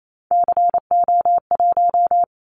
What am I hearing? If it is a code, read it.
Morse code, heard as a keyed tone, about 28 wpm, 700 Hz. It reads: CO1
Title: Callsign Prefixes for Top 100 DXCC - 28wpm